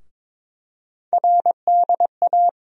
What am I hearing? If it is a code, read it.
Morse code, heard as a keyed tone, about 22 wpm, 700 Hz. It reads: RDA
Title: USA Call Sign Postfixes - 3 Letters 22wpm